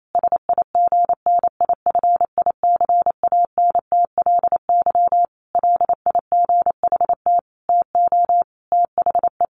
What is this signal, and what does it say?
Morse code, heard as a keyed tone, about 28 wpm, 700 Hz. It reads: SIGNIFICANTLY LIGHT TO THE
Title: Sentences from Top 1500 Words 28wpm